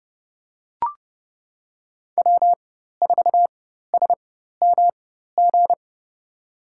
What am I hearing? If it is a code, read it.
Morse code, heard as a keyed tone, about 30 wpm, spaced as for 14 wpm, 700 Hz. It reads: W4SMG
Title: USA Call Signs - 5 Characters - 4x Character Spacing - 30wpm